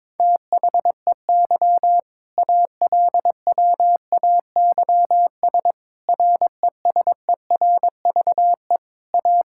Morse code, heard as a keyed tone, about 22 wpm, 700 Hz. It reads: THEY ALWAYS RESERVE A
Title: Sentences from Top 1000 Words - Review Part 6 - 22wpm